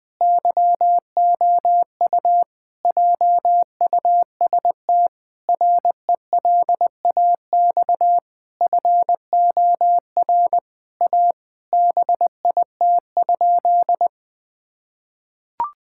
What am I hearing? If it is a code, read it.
Morse code, heard as a keyed tone, about 19 wpm, 700 Hz. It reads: YOU JUST RELAX FOR A BIT?
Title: Sentences with 5-Word Phrases - 20wpm